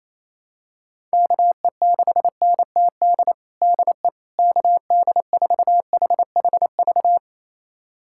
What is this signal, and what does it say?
Morse code, heard as a keyed tone, about 28 wpm, 700 Hz. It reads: KE6NTD DE KD4HHV
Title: USA Callsign DE Callsign 28wpm